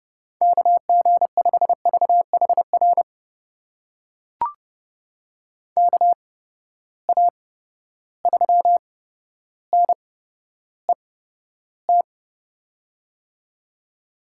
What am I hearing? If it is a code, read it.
Morse code, heard as a keyed tone, about 30 wpm, spaced as for 13 wpm, 700 Hz. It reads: KG5VHR KA3NET
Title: USA Call Signs - 6 Characters - 8x Character Spacing - 30wpm